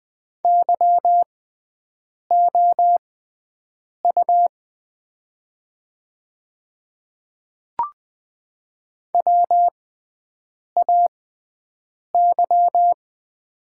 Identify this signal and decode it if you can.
Morse code, heard as a keyed tone, about 20 wpm, spaced as for 7 wpm, 700 Hz. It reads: YOU WAY
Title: 6x Character Spacing - Top 200 Words - 20wpm